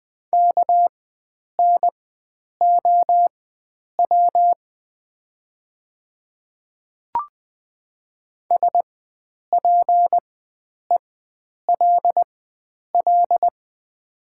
Morse code, heard as a keyed tone, about 20 wpm, spaced as for 9 wpm, 700 Hz. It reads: KNOW SPELL
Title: 4x Character Spacing - Top 300 Words - 20wpm